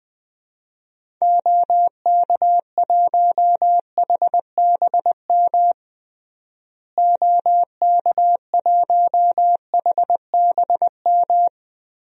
Morse code, heard as a keyed tone, about 20 wpm, 700 Hz. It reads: OK1HBM OK1HBM
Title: DXCC-100 Call Signs - 6 Characters - Repeated 2x - 20wpm